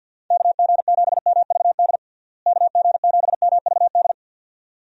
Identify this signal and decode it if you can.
Morse code, heard as a keyed tone, about 50 wpm, 700 Hz. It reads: KG7MVD KG7MVD
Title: USA Call Signs - 6 Characters - Repeated 2x - 50wpm